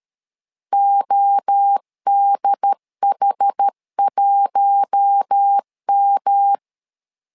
Morse code, heard as a keyed tone, about 13 wpm, 800 Hz. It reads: ODH1M
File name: LEZIONI / 4 Q1 12wpm 2028